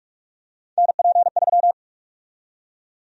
Morse code, heard as a keyed tone, about 46 wpm, 700 Hz. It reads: NP3
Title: USA Call Sign Prefixes - 45wpm